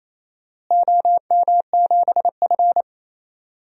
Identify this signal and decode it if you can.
Morse code, heard as a keyed tone, about 28 wpm, 700 Hz. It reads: OM7F
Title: DXCC-100 Call Signs - 4 Characters - Repeated 2x - 28wpm